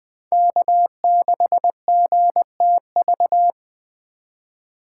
Morse code, heard as a keyed tone, about 21 wpm, 700 Hz. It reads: K6GTV